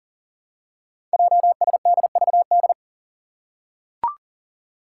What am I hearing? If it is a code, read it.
Morse code, heard as a keyed tone, about 42 wpm, 700 Hz. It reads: JSDVD
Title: Code Groups - 5 Letters - 40wpm